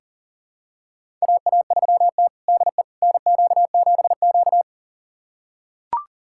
Morse code, heard as a keyed tone, about 40 wpm, 700 Hz. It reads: AA3T DE NQ7Q
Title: USA Callsign DE Callsign - 4 Characters - 40wpm